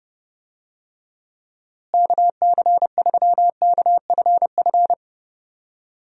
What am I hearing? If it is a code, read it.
Morse code, heard as a keyed tone, about 30 wpm, 700 Hz. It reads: KC3KFF
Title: USA Call Signs - 6 Characters - Repeated 2x - 30wpm